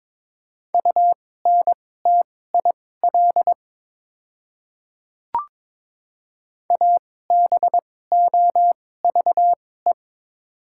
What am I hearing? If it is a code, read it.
Morse code, heard as a keyed tone, about 22 wpm, spaced as for 16 wpm, 690 Hz. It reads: UNTIL ABOVE
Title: 2x Character Spacing - Top 500 Words - 22wpm